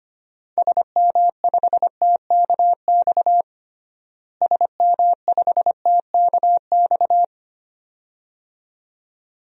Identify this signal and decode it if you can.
Morse code, heard as a keyed tone, about 25 wpm, 700 Hz. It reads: SM5TKX SM5TKX
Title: DXCC-100 Call Signs - 6 Characters - Repeated 2x - 25wpm